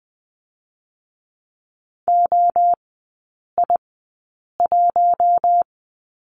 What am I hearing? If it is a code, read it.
Morse code, heard as a keyed tone, about 21 wpm, spaced as for 8 wpm, 700 Hz. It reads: OI1
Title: 087 Lesson 15e - Number 5 - Three Characters 20wpm